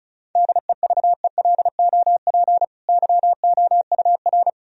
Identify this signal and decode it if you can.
Morse code, heard as a keyed tone, about 36 wpm, 700 Hz. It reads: DEVELOP YOUR
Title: Sentences from Top 1000 Words - Review Part 2 - 35wpm